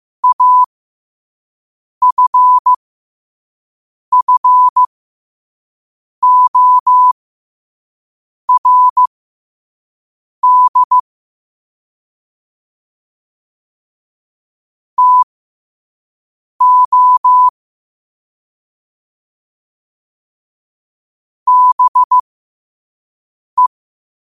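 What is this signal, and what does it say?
Morse code, heard as a keyed tone, about 15 wpm, spaced as for 5 wpm, 1000 Hz. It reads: AFFORD TO BE
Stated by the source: Morse code quote of the day at 5 wpm.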